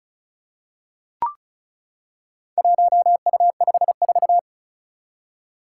Morse code, heard as a keyed tone, about 36 wpm, 700 Hz. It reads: 1U54
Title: Four Challenging Letter-Number Combinations 35wpm